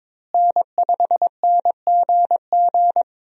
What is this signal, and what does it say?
Morse code, heard as a keyed tone, about 22 wpm, 700 Hz. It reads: N5NGG
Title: USA Call Signs - 5 Characters - Repeated 3x - 22wpm